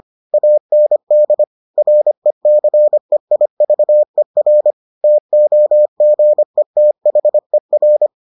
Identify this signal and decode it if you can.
Morse code, heard as a keyed tone, about 25 wpm, 600 Hz. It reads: AND RECEIVER TOGETHER